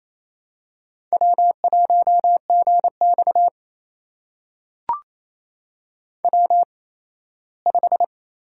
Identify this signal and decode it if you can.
Morse code, heard as a keyed tone, about 28 wpm, spaced as for 17 wpm, 700 Hz. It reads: W1GX W5
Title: USA Call Signs - 8x Character Spacing - 28wpm